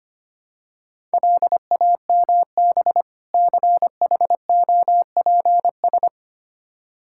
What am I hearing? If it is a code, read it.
Morse code, heard as a keyed tone, about 25 wpm, 700 Hz. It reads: LAMB CHOPS
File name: practice / 25wpm / Difficult Words with Hints - 25wpm